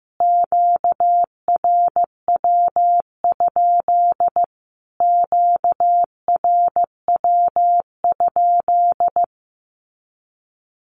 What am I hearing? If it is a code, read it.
Morse code, heard as a keyed tone, about 15 wpm, 700 Hz. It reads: QRW? QRW?
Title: Q-Codes Repeated 3x 15wpm